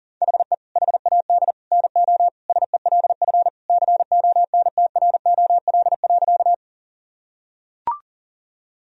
Morse code, heard as a keyed tone, about 40 wpm, 700 Hz. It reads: HE HAD NO SELF CONTROL.
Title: Sentences from Top 1000 Words - Review Part 3 - 40wpm